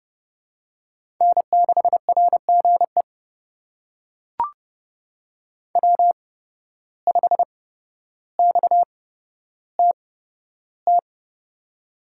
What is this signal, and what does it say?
Morse code, heard as a keyed tone, about 30 wpm, spaced as for 13 wpm, 700 Hz. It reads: N6RGE W5XTT
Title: USA Call Signs - 5 Characters - 8x Character Spacing - 30wpm